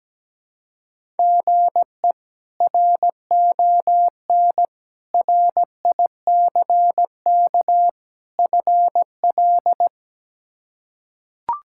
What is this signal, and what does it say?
Morse code, heard as a keyed tone, about 17 wpm, 700 Hz. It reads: GE RON RICK FL
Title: K1USN SST QSO Practice - 17wpm